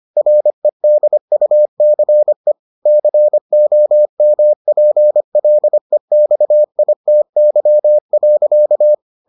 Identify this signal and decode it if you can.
Morse code, heard as a keyed tone, about 25 wpm, 600 Hz. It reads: REDUCE COMPLEXITY.